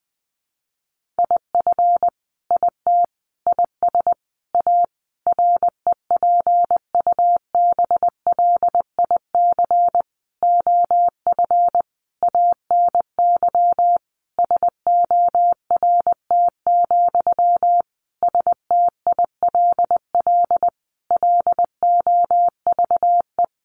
Morse code, heard as a keyed tone, about 20 wpm, 700 Hz. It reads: IF IT IS A REPUBLIC OF ANY SORT, STILL LOVE
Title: Sentences from Top 1500 Words 20wpm